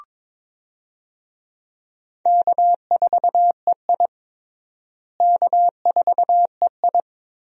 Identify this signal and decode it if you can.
Morse code, heard as a keyed tone, about 21 wpm, 700 Hz. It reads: K4EI K4EI
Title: USA Call Signs - 4 Characters - Repeated 3x - 22wpm